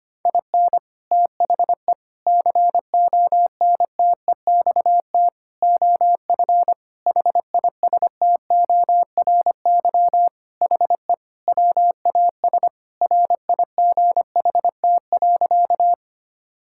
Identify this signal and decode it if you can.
Morse code, heard as a keyed tone, about 25 wpm, 700 Hz. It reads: IN THE CONTEXT OF HISTORY HE WAS RIGHT.
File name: practice / 25wpm / Sentences with 4-Word Phrases - 25wpm